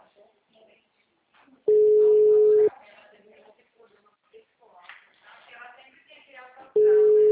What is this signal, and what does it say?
Ringback tone (Europe, 425 Hz), with speech.
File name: out- -2014- - - .1095